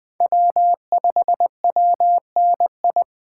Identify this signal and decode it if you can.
Morse code, heard as a keyed tone, about 20 wpm, 700 Hz. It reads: W5WNI
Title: USA Call Signs - 5 Characters - 3x Character Spacing - 20wpm